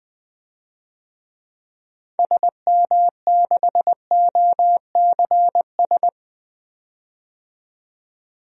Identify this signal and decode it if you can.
Morse code, heard as a keyed tone, about 20 wpm, 700 Hz. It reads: SM6OCS
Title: DXCC-100 Call Signs - 6 Characters - 20wpm